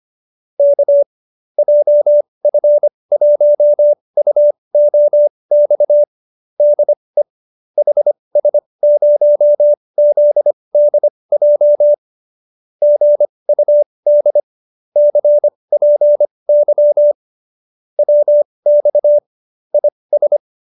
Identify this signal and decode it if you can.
Morse code, heard as a keyed tone, about 25 wpm, spaced as for 20 wpm, 590 Hz. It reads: K JF1UOX DE HS0ZDJ GUD CPY WX IS
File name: qso216b 20